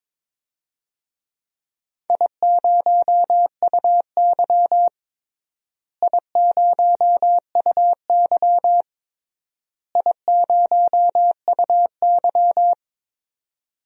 Morse code, heard as a keyed tone, about 22 wpm, 700 Hz. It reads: I0UY I0UY I0UY
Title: DXCC-100 Call Signs - 4 Characters - Repeated 3x - 22wpm